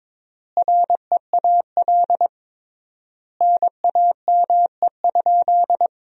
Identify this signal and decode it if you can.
Morse code, heard as a keyed tone, about 22 wpm, 700 Hz. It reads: REAL NAME?
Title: 3x Word-Spacing - Sentences from Top 300 Words - 22wpm